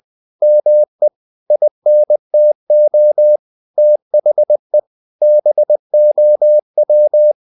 Morse code, heard as a keyed tone, about 20 wpm, 600 Hz. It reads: ME INTO THE BOW
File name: the old man and the sea / the old man and the sea part 1 20wpm